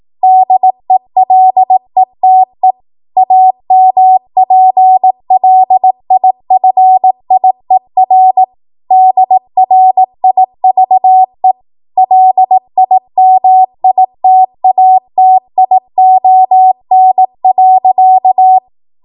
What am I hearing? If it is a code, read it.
Morse code, heard as a keyed tone, about 18 wpm, 750 Hz. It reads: DELETE AMPLIFIER DRIVE LIMITATION.
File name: Morse / 18WPM